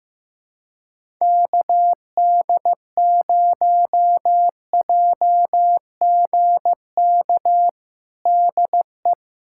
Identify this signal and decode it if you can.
Morse code, heard as a keyed tone, about 15 wpm, 700 Hz. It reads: KD0JGK DE
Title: USA Callsign DE Callsign - 6 Characters - 15wpm